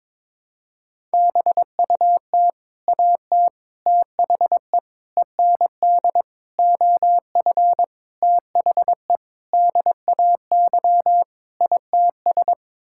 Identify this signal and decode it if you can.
Morse code, heard as a keyed tone, about 21 wpm, 700 Hz. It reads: BUT AT THE END OF THE DAY ITS
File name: practice / 22wpm / Sets of 8 Words - 22wpm